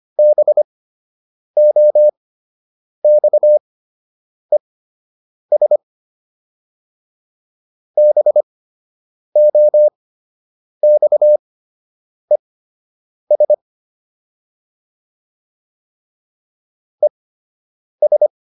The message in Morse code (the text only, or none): BOXES BOXES ES